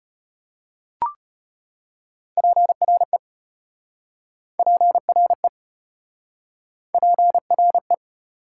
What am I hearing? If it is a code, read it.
Morse code, heard as a keyed tone, about 33 wpm, 700 Hz. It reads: PRE PRE PRE